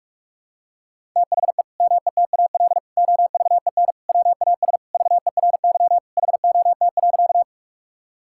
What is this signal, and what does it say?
Morse code, heard as a keyed tone, about 41 wpm, 700 Hz. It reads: THE METAL OVEN WAS VERY HOT.
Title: Sentences from Top 1000 Words - Review Part 6 - 45wpm